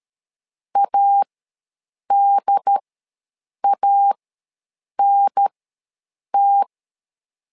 Morse code, heard as a keyed tone, about 13 wpm, spaced as for 7 wpm, 800 Hz. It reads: ADANT
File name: LEZIONI / 2 UHDN / 2 UHDN ws 12wpm 1708